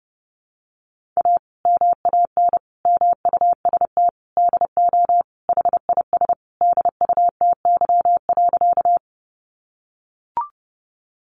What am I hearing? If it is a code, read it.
Morse code, heard as a keyed tone, about 30 wpm, 700 Hz. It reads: A MAN MUST DO HIS DUTY.
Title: Sentences from Top 1500 Words 30wpm